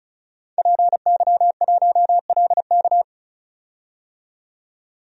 Morse code, heard as a keyed tone, about 36 wpm, 700 Hz. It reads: PY1LK
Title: DXCC-100 Call Signs - 5 Characters - 35wpm